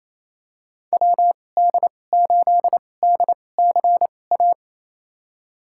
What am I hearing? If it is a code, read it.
Morse code, heard as a keyed tone, about 28 wpm, spaced as for 20 wpm, 700 Hz. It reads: WD8DCA
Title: USA Call Signs - 6 Characters - 2x Character Spacing - 28wpm